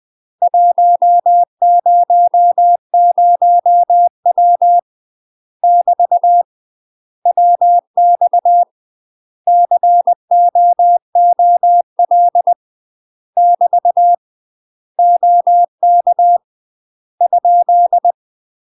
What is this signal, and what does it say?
Morse code, heard as a keyed tone, about 20 wpm, 700 Hz. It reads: 100W <BT> WX COOL <BT> OK ?
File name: UK MASTER QSOs TRAINING / 20WPM / UK MASTER QSOs 0048